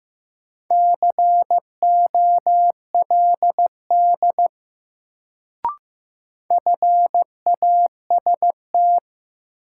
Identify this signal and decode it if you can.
Morse code, heard as a keyed tone, about 15 wpm, 700 Hz. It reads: COLD FAST
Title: Sentences from Top 400 Words - New Word Drill - 15wpm